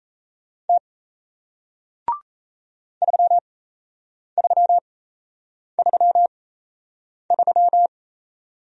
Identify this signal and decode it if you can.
Morse code, heard as a keyed tone, about 38 wpm, spaced as for 8 wpm, 700 Hz. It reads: T 3333